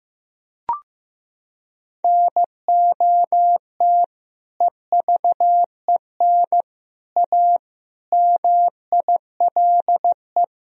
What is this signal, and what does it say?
Morse code, heard as a keyed tone, about 15 wpm, 700 Hz. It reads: NOT EVEN A MILE
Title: Sentences from Top 1000 Words - Review Part 1 - 15wpm